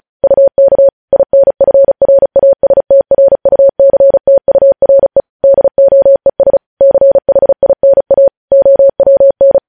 Morse code, heard as a keyed tone, about 36 wpm, 550 Hz. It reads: UK INFRASTRUCTURE DOES CHINA OWN
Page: news headlines in morse code at 35 wpm